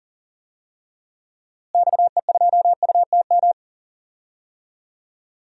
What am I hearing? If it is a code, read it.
Morse code, heard as a keyed tone, about 40 wpm, 700 Hz. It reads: XE2UTM
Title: DXCC-100 Call Signs - 6 Characters - Repeated 2x - 40wpm